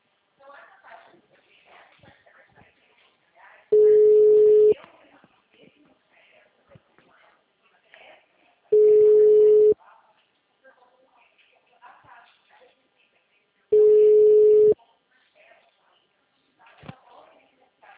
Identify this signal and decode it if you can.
Ringback tone (Europe, 425 Hz), with speech.